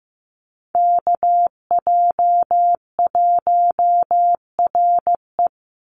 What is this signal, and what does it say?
Morse code, heard as a keyed tone, about 15 wpm, 700 Hz. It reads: KJ1RE
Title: Call Signs Repeated 3x - 15wpm